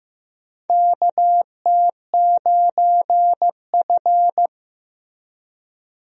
Morse code, heard as a keyed tone, about 15 wpm, 700 Hz. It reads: KT9F
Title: USA Call Signs - 4 Characters - Repeated 2x - 15wpm